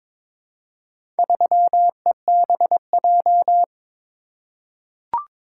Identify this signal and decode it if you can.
Morse code, heard as a keyed tone, about 22 wpm, 700 Hz. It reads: 3EBJ
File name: practice / 22wpm / Four Letter-Number Combinations - 22wpm